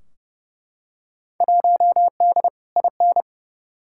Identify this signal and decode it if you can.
Morse code, heard as a keyed tone, about 30 wpm, 700 Hz. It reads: 1D IN
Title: Field Day Exchanges - 30wpm